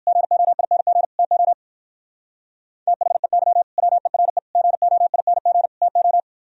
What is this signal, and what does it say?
Morse code, heard as a keyed tone, about 58 wpm, 700 Hz. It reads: GOING TO THEY WERE GOING TO